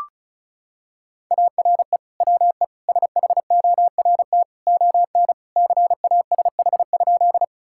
Morse code, heard as a keyed tone, about 36 wpm, 700 Hz. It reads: ARE WE SHORT ON CASH?
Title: Sentences from Top 1000 Words - Review Part 5 - 35wpm